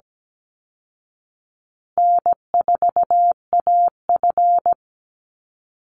Morse code, heard as a keyed tone, about 17 wpm, 700 Hz. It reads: N4AF
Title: Easy Call Signs Repeated 3x - 17wpm